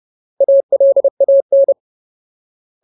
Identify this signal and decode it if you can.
Morse code, heard as a keyed tone, about 30 wpm, 550 Hz. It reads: ALAN
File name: Ultimate Word List / 30WPM Ultimate